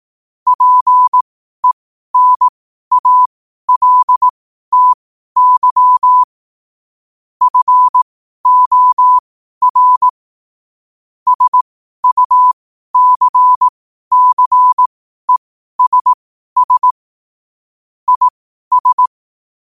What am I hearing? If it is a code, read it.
Morse code, heard as a keyed tone, about 18 wpm, spaced as for 12 wpm, 1000 Hz. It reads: PENALTY FOR SUCCESS IS